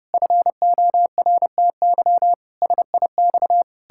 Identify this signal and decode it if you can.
Morse code, heard as a keyed tone, about 30 wpm, 700 Hz. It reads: FORTY SIX
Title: Pangram Challenge v2 - 30wpm